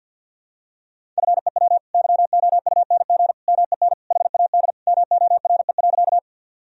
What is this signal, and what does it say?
Morse code, heard as a keyed tone, about 52 wpm, 700 Hz. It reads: FEW YOUNG MEN HAD MORE.
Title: Sentences from Top 1000 Words - Review Part 1 - 50wpm